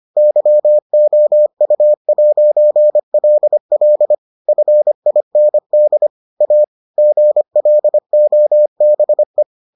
Morse code, heard as a keyed tone, about 25 wpm, 600 Hz. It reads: YOU'LL FIND A GLOBE